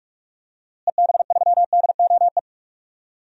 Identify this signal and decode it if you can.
Morse code, heard as a keyed tone, about 46 wpm, 700 Hz. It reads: EB3DOE
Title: DXCC-100 Call Signs - 6 Characters - Repeated 3x - 45wpm